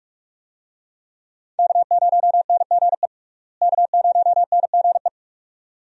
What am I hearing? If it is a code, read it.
Morse code, heard as a keyed tone, about 41 wpm, 700 Hz. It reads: K0NGE K0NGE
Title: USA Call Signs - 5 Characters - Repeated 2x - 45wpm